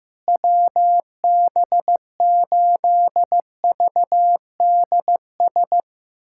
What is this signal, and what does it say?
Morse code, heard as a keyed tone, about 15 wpm, 700 Hz. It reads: WB8VDS
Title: USA Callsign DE Callsign - 6 Characters - 15wpm